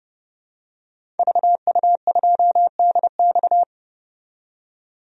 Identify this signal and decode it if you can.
Morse code, heard as a keyed tone, about 30 wpm, 700 Hz. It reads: VU2DX
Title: DXCC-100 Call Signs - 5 Characters - 30wpm